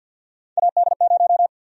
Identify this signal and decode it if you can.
Morse code, heard as a keyed tone, about 52 wpm, 700 Hz. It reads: AD0
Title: USA Call Sign Prefixes - 50wpm